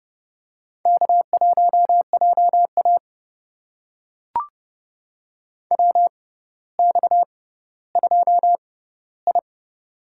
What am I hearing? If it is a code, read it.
Morse code, heard as a keyed tone, about 31 wpm, spaced as for 16 wpm, 700 Hz. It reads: K1JA WX2I
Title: USA Call Signs - 4 Characters - 6x Character Spacing - 30wpm